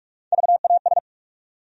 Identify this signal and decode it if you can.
Morse code, heard as a keyed tone, about 46 wpm, 700 Hz. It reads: VAS
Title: USA Call Sign Postfixes - 3 Letters 45wpm